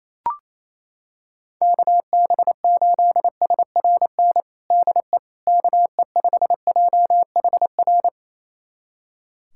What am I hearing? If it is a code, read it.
Morse code, heard as a keyed tone, about 28 wpm, 700 Hz. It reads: KB8SRN DE KE5JHR
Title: USA Callsign DE Callsign - 6 Characters - 28wpm